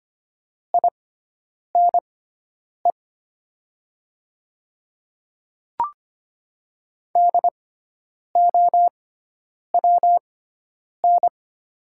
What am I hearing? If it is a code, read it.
Morse code, heard as a keyed tone, about 25 wpm, spaced as for 9 wpm, 700 Hz. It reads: INE DOWN
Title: 6x Character Spacing - Top 500 Words - 25wpm